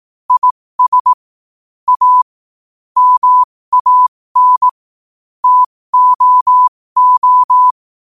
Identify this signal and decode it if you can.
Morse code, heard as a keyed tone, about 18 wpm, spaced as for 15 wpm, 1000 Hz. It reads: IS A MAN TOO
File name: qotdfiles / 15wpm